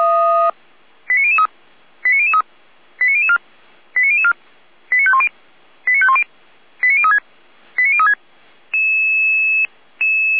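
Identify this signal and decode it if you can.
Five-tone selective call (ZVEI), heard as quick runs of five tones, tones of about 70 ms. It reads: 89003 89003 89004 89004 89510 89510 89036 89036
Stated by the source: Five-tone ZVEI selective call 89001, received by SDR, from fire brigade Orlamünde (feuerwehr-orlamuende).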